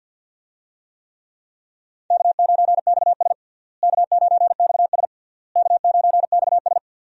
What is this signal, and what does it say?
Morse code, heard as a keyed tone, about 51 wpm, 700 Hz. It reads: K9XS K9XS K9XS
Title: USA Call Signs - 4 Characters - Repeated 3x - 50wpm